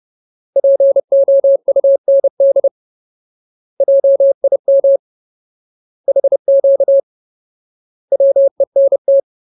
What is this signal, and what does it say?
Morse code, heard as a keyed tone, about 30 wpm, 550 Hz. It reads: POUND JIM HQ WENT
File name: Ultimate Word List / 30WPM Ultimate